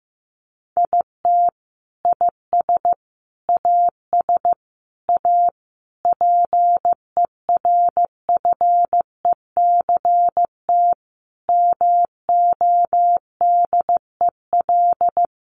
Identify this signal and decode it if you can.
Morse code, heard as a keyed tone, about 15 wpm, 700 Hz. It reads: IT IS AS A PERFECT MODEL
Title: Sentences from Top 1000 Words 15wpm